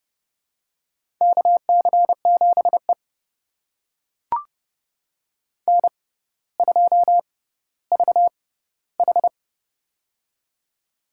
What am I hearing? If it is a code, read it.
Morse code, heard as a keyed tone, about 31 wpm, spaced as for 15 wpm, 700 Hz. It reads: KC7E N2VH
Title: USA Call Signs - 4 Characters - 6x Character Spacing - 30wpm